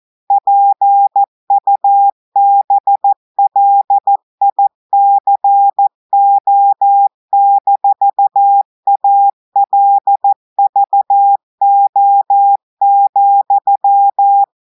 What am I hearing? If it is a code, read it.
Morse code, heard as a keyed tone, about 14 wpm, 800 Hz. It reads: PUBLICO-ALVO,
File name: audios / portugais / portugais texte en clair 2 vitesse 14 mots minutes